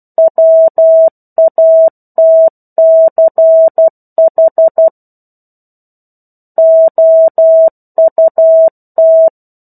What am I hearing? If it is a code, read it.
Morse code, heard as a keyed tone, about 12 wpm, 650 Hz. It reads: WATCH OUT
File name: 01-Chapter-01 12 WPM